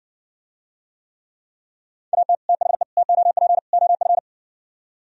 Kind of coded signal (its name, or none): Morse code